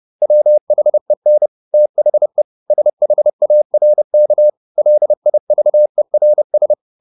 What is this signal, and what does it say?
Morse code, heard as a keyed tone, about 30 wpm, 600 Hz. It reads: WHEN THE SHARK LIVERS
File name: the old man and the sea / the old man and the sea 1 30wpm